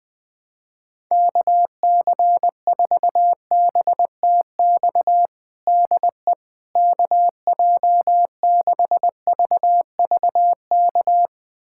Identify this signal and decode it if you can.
Morse code, heard as a keyed tone, about 21 wpm, 700 Hz. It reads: KC4BTX DE KJ6VVK
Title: USA Callsign DE Callsign - 6 Characters - 20wpm